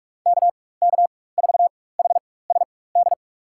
Morse code, heard as a keyed tone, about 46 wpm, spaced as for 21 wpm, 700 Hz. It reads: KK4HSD
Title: USA Call Signs - 4x Character Spacing - 45wpm